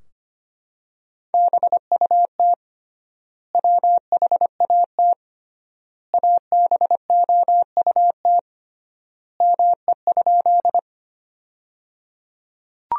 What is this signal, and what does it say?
Morse code, heard as a keyed tone, about 25 wpm, 700 Hz. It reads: BUT WHAT ABOUT ME?
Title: 3x Word-Spacing - Sentences from Top 200 Words - 25wpm